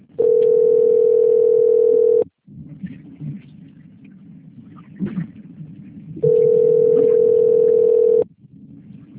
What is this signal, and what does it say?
Ringback tone (North America), 440 + 480 Hz.